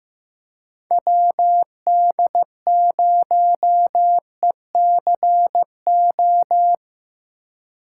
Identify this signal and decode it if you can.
Morse code, heard as a keyed tone, about 15 wpm, 700 Hz. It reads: WD0ECO
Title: K1USN SST QSO Practice - 15wpm